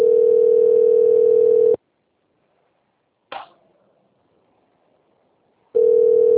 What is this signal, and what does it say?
Ringback tone (North America), 440 + 480 Hz.
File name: out- - - - - .93128